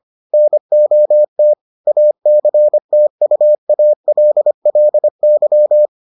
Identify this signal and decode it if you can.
Morse code, heard as a keyed tone, about 25 wpm, 600 Hz. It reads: NOT ACTUALLY